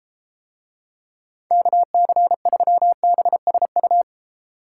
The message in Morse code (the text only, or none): KC3BSU